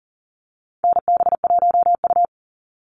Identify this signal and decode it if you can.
Morse code, heard as a keyed tone, about 40 wpm, 700 Hz. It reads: NB1U
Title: Call Signs Repeated 3x - 40wpm